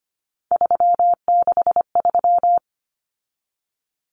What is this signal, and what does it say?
Morse code, heard as a keyed tone, about 25 wpm, 700 Hz. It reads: 363
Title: Common Cut Numbers - 3-Digit 25wpm